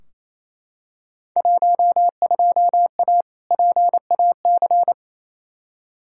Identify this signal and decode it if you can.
Morse code, heard as a keyed tone, about 28 wpm, 700 Hz. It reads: 12A PAC
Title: Field Day Exchanges Encore - 28wpm